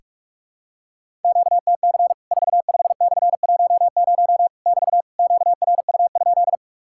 Morse code, heard as a keyed tone, about 44 wpm, 700 Hz. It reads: QTC 45/10 <BT> QRU?
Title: Worked All Europe DX Contest - QTC Exchange - 45wpm